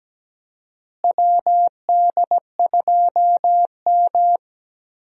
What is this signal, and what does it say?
Morse code, heard as a keyed tone, about 17 wpm, 700 Hz. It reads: WD2M
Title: USA Call Signs - 4 Characters - Repeated 3x - 17wpm